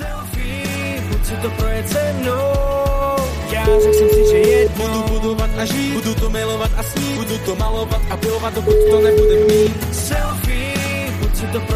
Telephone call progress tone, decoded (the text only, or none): ringback tone